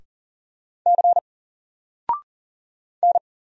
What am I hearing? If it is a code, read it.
Morse code, heard as a keyed tone, about 39 wpm, 700 Hz. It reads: C N
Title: Single Character (Letter:Number:Punctuation:Prosign) Alt 40wpm